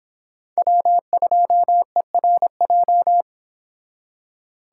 Morse code, heard as a keyed tone, about 25 wpm, 700 Hz. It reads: W2ERJ